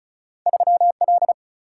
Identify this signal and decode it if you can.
Morse code, heard as a keyed tone, about 36 wpm, 700 Hz. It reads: 3 L
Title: Single-Letter Single-Number 35wpm